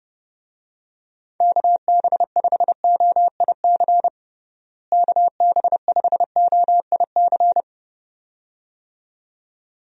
Morse code, heard as a keyed tone, about 29 wpm, 700 Hz. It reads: KB5OIC KB5OIC
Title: USA Call Signs - 6 Characters - Repeated 2x - 30wpm